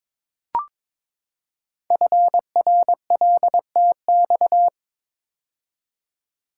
Morse code, heard as a keyed tone, about 22 wpm, 700 Hz. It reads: FRLTX
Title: Code Groups - 5 Letters - 22wpm